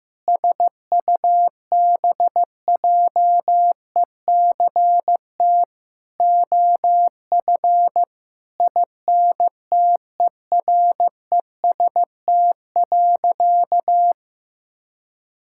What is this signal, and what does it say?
Morse code, heard as a keyed tone, about 15 wpm, 690 Hz. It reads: SUBJECT OF INTEREST.
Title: Sentences from Top 1000 Words - Review Part 5 - 15wpm